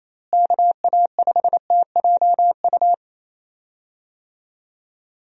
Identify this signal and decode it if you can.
Morse code, heard as a keyed tone, about 28 wpm, 700 Hz. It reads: KA5TJU
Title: USA Call Signs - 6 Characters - Repeated 2x - 28wpm